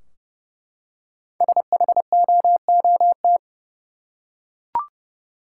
Morse code, heard as a keyed tone, about 29 wpm, 700 Hz. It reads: SHOOT
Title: Five Letter Words - 30wpm